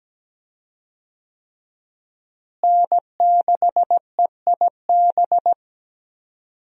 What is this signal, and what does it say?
Morse code, heard as a keyed tone, about 17 wpm, 700 Hz. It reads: N6EIB
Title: USA Call Signs - 5 Characters - Repeated 3x - 17wpm